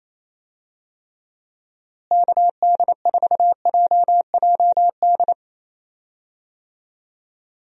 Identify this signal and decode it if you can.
Morse code, heard as a keyed tone, about 28 wpm, 700 Hz. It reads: KD4JJD
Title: USA Call Signs - 6 Characters - Repeated 2x - 28wpm